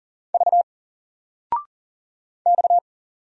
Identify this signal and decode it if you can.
Morse code, heard as a keyed tone, about 39 wpm, 700 Hz. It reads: V X
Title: Single Letter-Number Alt 40wpm